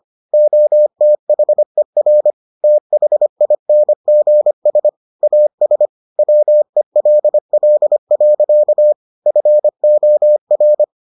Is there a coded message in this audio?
Yes: Morse code from 0.3 to 10.8 s, heard as a keyed tone, about 25 wpm, 600 Hz. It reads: OTHER THINGS AS WELL. FOR